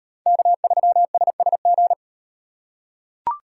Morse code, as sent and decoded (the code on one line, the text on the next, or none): -.- ...-- ... ... --.
K3SSG